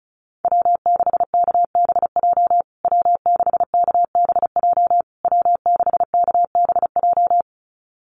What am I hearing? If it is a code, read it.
Morse code, heard as a keyed tone, about 34 wpm, 700 Hz. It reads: W6KBJ W6KBJ W6KBJ
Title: USA Call Signs Repeated 3x 35wpm